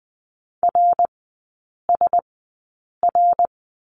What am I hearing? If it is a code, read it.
Morse code, heard as a keyed tone, about 20 wpm, spaced as for 8 wpm, 700 Hz. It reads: RSR
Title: 053 Lesson 10e - Letter r - Three Characters 20wpm